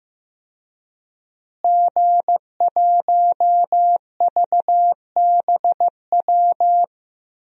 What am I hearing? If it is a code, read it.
Morse code, heard as a keyed tone, about 15 wpm, 700 Hz. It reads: G1VBW
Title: DXCC-100 Call Signs - 5 Characters - Repeated 2x - 15wpm